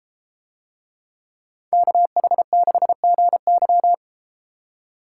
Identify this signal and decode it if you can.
Morse code, heard as a keyed tone, about 32 wpm, 700 Hz. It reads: KH6GY